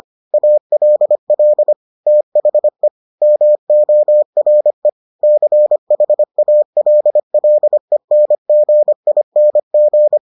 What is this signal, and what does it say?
Morse code, heard as a keyed tone, about 25 wpm, 600 Hz. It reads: ALL THE MORE CHALLENGING